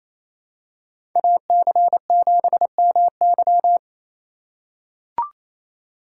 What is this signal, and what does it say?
Morse code, heard as a keyed tone, about 28 wpm, 700 Hz. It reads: AC7MY
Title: USA Call Signs - 5 Characters - 2x Character Spacing - 28wpm